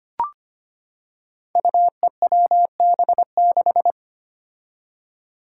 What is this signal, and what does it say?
Morse code, heard as a keyed tone, about 25 wpm, 700 Hz. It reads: UEWB6
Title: Five Challenging Letter-Number Combinations 25wpm